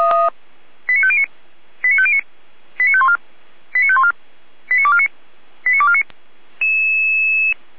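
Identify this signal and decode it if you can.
Five-tone selective call (ZVEI), heard as quick runs of five tones, tones of about 70 ms. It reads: 89509 89509 89514 89514 89249 89249
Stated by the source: ZVEI five-tone selective call 89250 received on SDR, from feuerwehr-orlamuende fire brigade.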